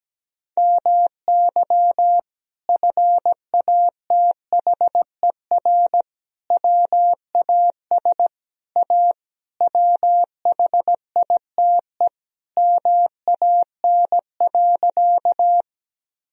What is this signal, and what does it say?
Morse code, heard as a keyed tone, about 17 wpm, 700 Hz. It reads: MY FATHER WAS A WHITE MAN.
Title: Sentences from Top 1000 Words - Review Part 1 - 17wpm